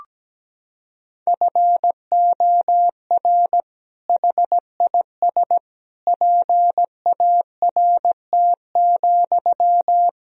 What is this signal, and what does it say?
Morse code, heard as a keyed tone, about 17 wpm, 700 Hz. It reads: FOR HIS PART,
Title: Sentences from Top 1000 Words - Review Part 2 - 17wpm